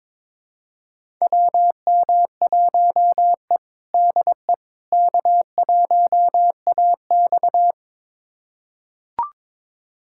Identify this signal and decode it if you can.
Morse code, heard as a keyed tone, about 21 wpm, 700 Hz. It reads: WM1E DE K1AX
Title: USA Callsign DE Callsign - 4 Characters - 22wpm